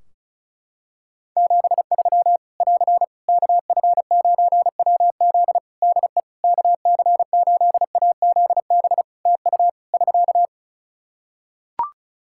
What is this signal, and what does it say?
Morse code, heard as a keyed tone, about 34 wpm, 700 Hz. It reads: 73 <AR> KF9WZ DE KC8AZB TU <SK>
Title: Practice QSO - Exchange 4 - Ending - 35wpm